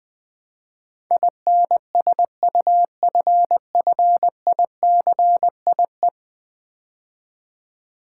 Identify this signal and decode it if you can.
Morse code, heard as a keyed tone, about 20 wpm, 700 Hz. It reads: INSUFFICIE
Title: Build A Word - 5 Letters At A Time - 12-Letter Words - 20wpm